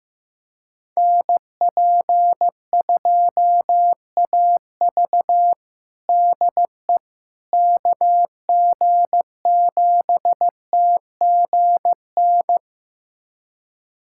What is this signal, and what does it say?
Morse code, heard as a keyed tone, about 15 wpm, 700 Hz. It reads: NP2AV DE KG7TGN
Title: USA Callsign DE Callsign 15wpm